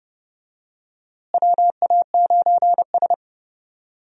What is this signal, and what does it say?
Morse code, heard as a keyed tone, about 32 wpm, 700 Hz. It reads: WA9S
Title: USA Call Signs - 4 Characters - Repeated 3x - 30wpm